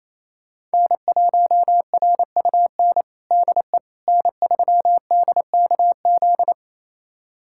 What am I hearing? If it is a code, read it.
Morse code, heard as a keyed tone, about 28 wpm, 700 Hz. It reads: N1RUN DE N3DKZ
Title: USA Callsign DE Callsign - 5 Characters - 28wpm